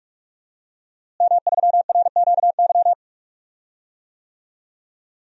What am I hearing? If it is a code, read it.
Morse code, heard as a keyed tone, about 42 wpm, 700 Hz. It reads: M3RQY
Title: DXCC-100 Call Signs - 5 Characters - Repeated 2x - 45wpm